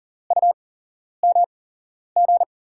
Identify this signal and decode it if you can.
Morse code, heard as a keyed tone, about 40 wpm, spaced as for 11 wpm, 700 Hz. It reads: UMG
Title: USA Call Signs - 8x Character Spacing - 40wpm